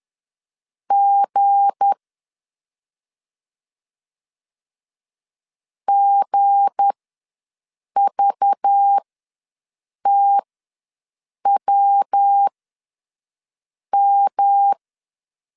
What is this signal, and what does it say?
Morse code, heard as a keyed tone, about 10 wpm, spaced as for 6 wpm, 800 Hz. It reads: G GVTWM
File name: LEZIONI / 7 GF9Z / 7 G ws 10wpm 3843